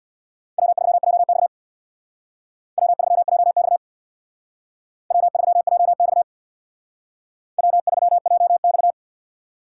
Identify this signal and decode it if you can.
Morse code, heard as a keyed tone, about 63 wpm, 700 Hz. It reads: W3JX W3JX W3JX W3JX